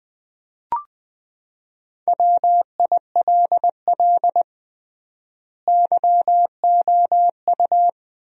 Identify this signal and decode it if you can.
Morse code, heard as a keyed tone, about 21 wpm, 700 Hz. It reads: WILL YOU
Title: 3x Word-Spacing - Sentences from Top 300 Words - 20wpm